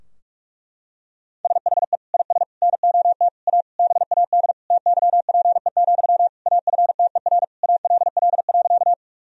Morse code, heard as a keyed tone, about 46 wpm, 700 Hz. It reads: SHE IS NOT A BAD TYPE, AFTER ALL.
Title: Sentences from Top 1000 Words - Review Part 3 - 45wpm